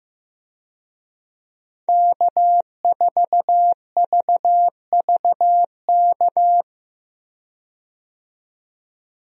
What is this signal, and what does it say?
Morse code, heard as a keyed tone, about 15 wpm, 700 Hz. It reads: K4VVK
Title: USA Call Signs - 5 Characters - 15wpm